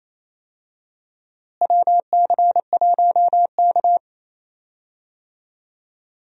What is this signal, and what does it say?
Morse code, heard as a keyed tone, about 28 wpm, 700 Hz. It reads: WC1K
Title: USA Call Signs - 4 Characters - Repeated 2x - 28wpm